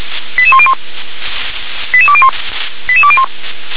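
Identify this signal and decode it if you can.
Five-tone selective call (ZVEI), heard as quick runs of five tones, tones of about 70 ms. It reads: 88101 88201 88201
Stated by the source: ZVEI five-tone selective call 88119 received by SDR.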